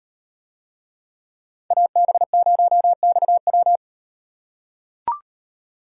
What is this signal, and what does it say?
Morse code, heard as a keyed tone, about 38 wpm, 700 Hz. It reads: AB0XW